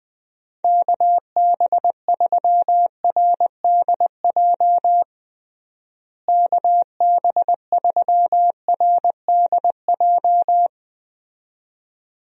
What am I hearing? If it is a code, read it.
Morse code, heard as a keyed tone, about 20 wpm, 700 Hz. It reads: KB3RDJ KB3RDJ
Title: USA Call Signs - 6 Characters - Repeated 2x - 20wpm